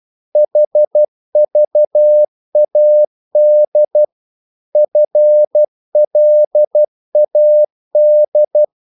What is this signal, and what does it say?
Morse code, heard as a keyed tone, about 12 wpm, 600 Hz. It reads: HVAD FLAD
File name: Korte ord DK 12wpm